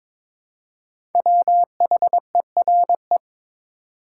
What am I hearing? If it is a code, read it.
Morse code, heard as a keyed tone, about 22 wpm, 700 Hz. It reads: WHERE
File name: 5 Letter Words - First 3 Letters Given - 22wpm